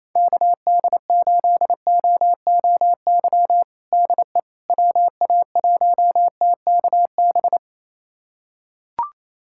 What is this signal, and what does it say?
Morse code, heard as a keyed tone, about 28 wpm, 700 Hz. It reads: KD8OOY DE WA1TKB
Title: USA Callsign DE Callsign - 6 Characters - 28wpm